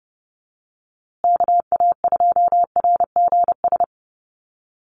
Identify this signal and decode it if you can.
Morse code, heard as a keyed tone, about 30 wpm, 700 Hz. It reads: KA2RGS
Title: USA Call Signs Repeated 3x 30wpm